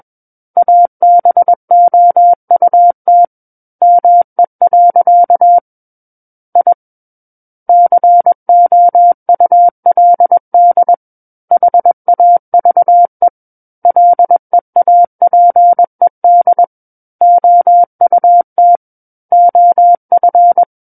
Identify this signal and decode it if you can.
Morse code, heard as a keyed tone, about 21 wpm, 700 Hz. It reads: ABOUT ME. I COULD HAVE LEAPED OUT OF
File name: Book One / 12-Chapter 12 - 21 WPM